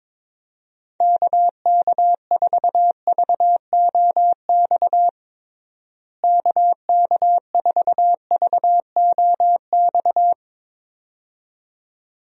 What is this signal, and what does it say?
Morse code, heard as a keyed tone, about 21 wpm, 700 Hz. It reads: KK4VOX KK4VOX
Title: USA Call Signs - 6 Characters - Repeated 2x - 22wpm